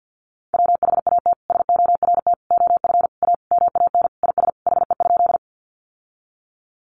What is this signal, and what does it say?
Morse code, heard as a keyed tone, about 50 wpm, 700 Hz. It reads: WHAT SORT OF A MAN IS HE?
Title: Sentences from Top 1300 Words 50wpm